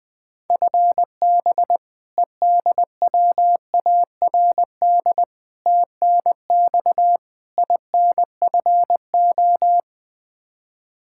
Morse code, heard as a keyed tone, about 20 wpm, 700 Hz. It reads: FB EDWARD TNX INFO
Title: Practice QSO - Exchange 3 - More Info - 20wpm